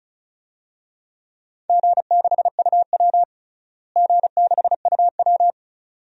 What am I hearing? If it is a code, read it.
Morse code, heard as a keyed tone, about 36 wpm, 700 Hz. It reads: G6UW G6UW
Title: DXCC-100 Call Signs - 4 Characters - Repeated 2x - 35wpm